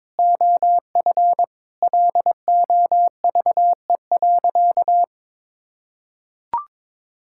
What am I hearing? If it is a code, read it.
Morse code, heard as a keyed tone, about 21 wpm, 700 Hz. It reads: OF LOVE.
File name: practice / 22wpm / Sentences with 4-Word Phrases - 22wpm